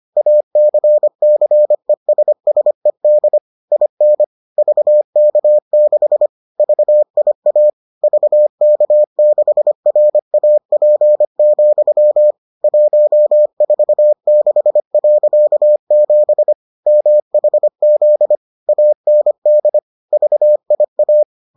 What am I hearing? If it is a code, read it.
Morse code, heard as a keyed tone, about 25 wpm, 600 Hz. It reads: ACCESSED IN VK6 VIA VK6RAP, 146.7 MHZ AND VIA